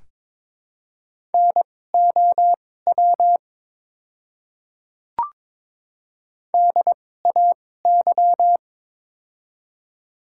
Morse code, heard as a keyed tone, about 22 wpm, spaced as for 16 wpm, 700 Hz. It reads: NOW DAY
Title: 2x Character Spacing - Top 200 Words - 22wpm